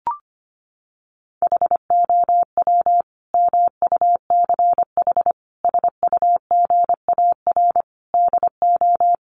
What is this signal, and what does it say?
Morse code, heard as a keyed tone, about 25 wpm, 700 Hz. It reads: HOW MUCH SUGAR DO
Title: Sentences from Top 1200 Words 25wpm